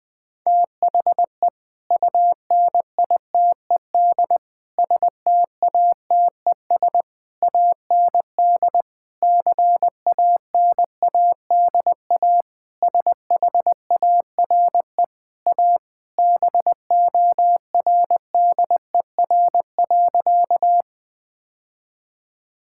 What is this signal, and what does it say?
Morse code, heard as a keyed tone, about 20 wpm, 700 Hz. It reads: THE UNITED STATES AND CANADA SHARE A BORDER.
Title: Sentences with 5-Word Phrases - 20wpm